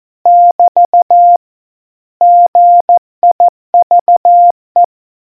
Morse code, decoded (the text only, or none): <BT> GIVE